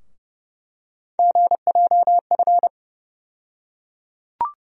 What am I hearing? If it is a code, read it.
Morse code, heard as a keyed tone, about 30 wpm, 700 Hz. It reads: GJF
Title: USA Call Sign Postfixes - 3 Letters 30wpm